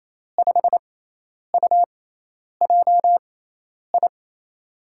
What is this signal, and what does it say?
Morse code, heard as a keyed tone, about 28 wpm, spaced as for 10 wpm, 700 Hz. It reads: 5UJI
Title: USA Call Signs - 6 Characters - 6x Character Spacing - 28wpm